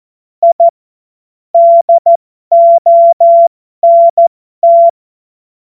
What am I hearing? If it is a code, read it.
Morse code, heard as a keyed tone, about 14 wpm, spaced as for 12 wpm, 680 Hz. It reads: I DONT
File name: Morsekurs / Witze